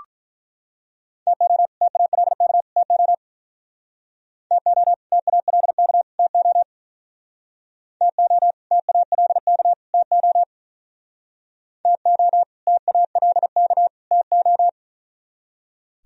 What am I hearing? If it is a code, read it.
Morse code, heard as a keyed tone, about 44 wpm, 700 Hz. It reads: TO TALK TO TO TALK TO TO TALK TO TO TALK TO